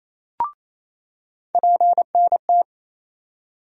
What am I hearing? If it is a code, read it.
Morse code, heard as a keyed tone, about 28 wpm, 700 Hz. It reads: PNT
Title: USA Call Sign Postfixes - 3 Letters 28wpm